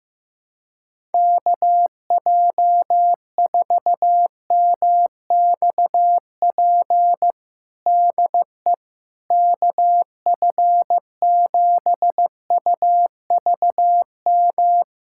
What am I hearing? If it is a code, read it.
Morse code, heard as a keyed tone, about 16 wpm, 700 Hz. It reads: KJ4MXP DE KF7UVM
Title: USA Callsign DE Callsign 15wpm